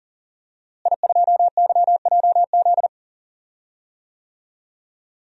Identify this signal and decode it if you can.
Morse code, heard as a keyed tone, about 38 wpm, 700 Hz. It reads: I2YJZ
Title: DXCC-100 Call Signs - 5 Characters - Repeated 3x - 40wpm